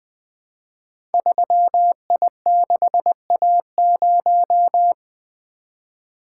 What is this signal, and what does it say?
Morse code, heard as a keyed tone, about 20 wpm, 700 Hz. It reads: 3I6A0
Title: Code Groups - 5 Letter:Numbers - 20wpm